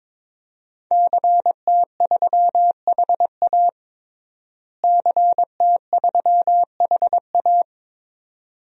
Morse code, heard as a keyed tone, about 21 wpm, 700 Hz. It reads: CT3HA CT3HA
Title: DXCC-100 Call Signs - 5 Characters - Repeated 2x - 22wpm